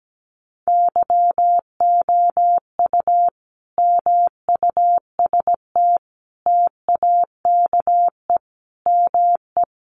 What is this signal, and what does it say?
Morse code, heard as a keyed tone, about 17 wpm, 700 Hz. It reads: YOU MUST TAKE ME
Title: Sentences from Top 1400 Words 17wpm